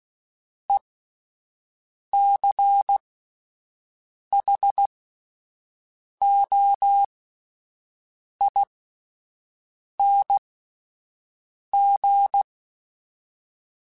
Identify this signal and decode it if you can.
Morse code, heard as a keyed tone, about 16 wpm, spaced as for 5 wpm, 800 Hz. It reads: ECHOING
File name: Alphabetical / Alphabetical mp3s / Es @5